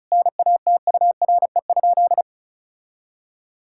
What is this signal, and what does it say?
Morse code, heard as a keyed tone, about 34 wpm, 700 Hz. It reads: NATURE?
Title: Sentences from Top 1000 Words - Review Part 6 - 35wpm